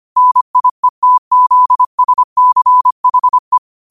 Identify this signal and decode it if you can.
Morse code, heard as a keyed tone, about 25 wpm, 1000 Hz. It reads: NIETZSCHE